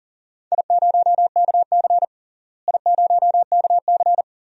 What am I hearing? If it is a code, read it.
Morse code, heard as a keyed tone, about 37 wpm, 700 Hz. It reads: I0KC I0KC
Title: DXCC-100 Call Signs - 4 Characters - Repeated 2x - 40wpm